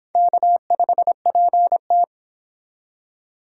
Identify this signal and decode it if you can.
Morse code, heard as a keyed tone, about 25 wpm, 700 Hz. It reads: K5PT